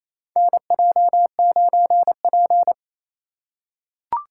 Morse code, heard as a keyed tone, about 28 wpm, 700 Hz. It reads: NJ9P
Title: USA Call Signs - 4x Character Spacing - 28wpm